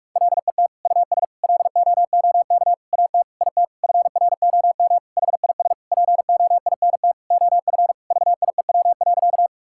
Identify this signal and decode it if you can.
Morse code, heard as a keyed tone, about 46 wpm, 700 Hz. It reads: LET US LOOK AT IT FROM HIS POINT OF VIEW.
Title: Sentences from Top 1000 Words - Review Part 6 - 45wpm